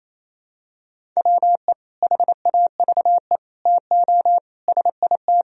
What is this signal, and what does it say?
Morse code, heard as a keyed tone, about 28 wpm, 700 Hz. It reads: WE HAVE TO SIT
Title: Sentences from Top 1000 Words - Review Part 1 - 28wpm